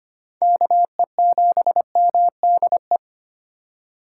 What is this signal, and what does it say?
Morse code, heard as a keyed tone, about 25 wpm, 700 Hz. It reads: KE7MDE
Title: USA Call Signs - 2x Character Spacing - 25wpm